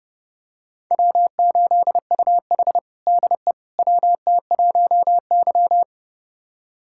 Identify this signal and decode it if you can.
Morse code, heard as a keyed tone, about 30 wpm, 700 Hz. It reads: W8UH DE WT1Y
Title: USA Callsign DE Callsign - 4 Characters - 30wpm